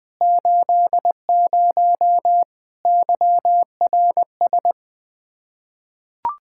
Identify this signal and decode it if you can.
Morse code, heard as a keyed tone, about 21 wpm, 700 Hz. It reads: 80 YRS
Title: K1USN SST QSO Practice 20wpm